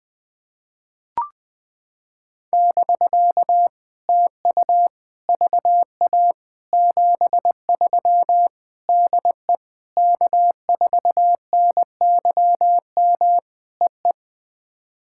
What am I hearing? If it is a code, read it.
Morse code, heard as a keyed tone, about 21 wpm, 700 Hz. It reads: <BK> TU VA 73 DE K4NYM EE
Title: POTA QSO Practice - 20wpm